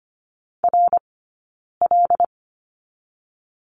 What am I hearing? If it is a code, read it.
Morse code, heard as a keyed tone, about 24 wpm, 700 Hz. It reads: R L
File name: Lesson 15d - Number 5 - Two Characters 25wpm